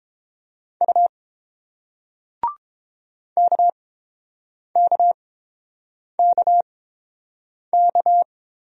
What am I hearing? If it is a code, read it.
Morse code, heard as a keyed tone, about 30 wpm, spaced as for 7 wpm, 700 Hz. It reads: U KKKK